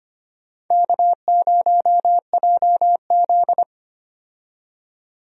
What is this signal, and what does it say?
Morse code, heard as a keyed tone, about 25 wpm, 700 Hz. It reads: K0JZ
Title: USA Call Signs - 4 Characters - 25wpm